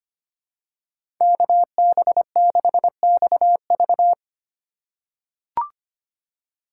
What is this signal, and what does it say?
Morse code, heard as a keyed tone, about 25 wpm, 700 Hz. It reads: KB6XV
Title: USA Call Signs - 5 Characters - 4x Character Spacing - 25wpm